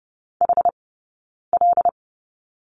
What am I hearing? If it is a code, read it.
Morse code, heard as a keyed tone, about 30 wpm, spaced as for 9 wpm, 690 Hz. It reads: HL
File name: Lesson 17d - Letter u - Two Characters 30wpm